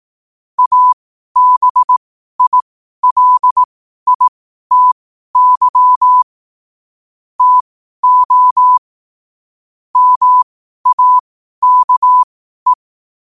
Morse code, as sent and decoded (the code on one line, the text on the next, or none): .- -... .. .-.. .. - -.-- / - --- / -- .- -.- .
ABILITY TO MAKE